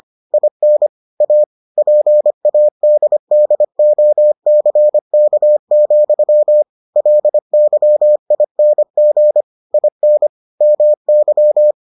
Morse code, heard as a keyed tone, about 25 wpm, 600 Hz. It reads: IN A PADDOCK, LYING IN MY